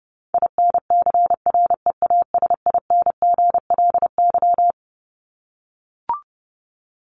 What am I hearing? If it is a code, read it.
Morse code, heard as a keyed tone, about 30 wpm, 700 Hz. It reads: INCREASINGLY
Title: LY Suffix - Top 100 Words with Base Words 30wpm